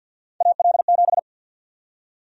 Morse code, heard as a keyed tone, about 52 wpm, 700 Hz. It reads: AL7
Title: USA Call Sign Prefixes - Repeated 2x - 50wpm